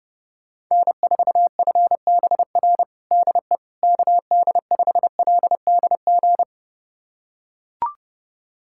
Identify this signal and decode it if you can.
Morse code, heard as a keyed tone, about 30 wpm, 700 Hz. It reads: N4FBR DE KD5LDG
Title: USA Callsign DE Callsign 30wpm